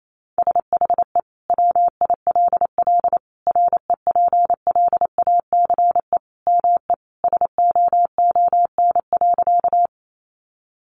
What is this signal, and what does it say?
Morse code, heard as a keyed tone, about 28 wpm, 700 Hz. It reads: SHE WILL REPLACE ME SOON.
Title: Sentences from Top 1400 Words 28wpm